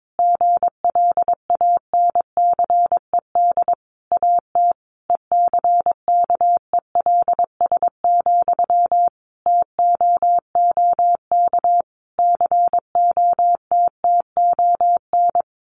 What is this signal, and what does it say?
Morse code, heard as a keyed tone, about 22 wpm, 700 Hz. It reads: GLANCED AT ECKELS, TOOK COTTON
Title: A Sound Of Thunder 22wpm